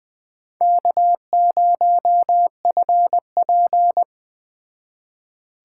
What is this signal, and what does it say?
Morse code, heard as a keyed tone, about 20 wpm, 700 Hz. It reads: K0FP
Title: Speed Racing - USA 1x2 Call Signs 17wpm